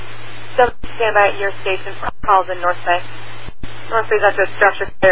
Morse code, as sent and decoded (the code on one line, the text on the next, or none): none